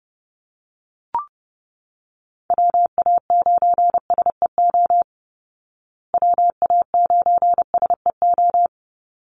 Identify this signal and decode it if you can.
Morse code, heard as a keyed tone, about 30 wpm, 700 Hz. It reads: WA9SEO WA9SEO
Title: Call Signs Repeated 2x - 30wpm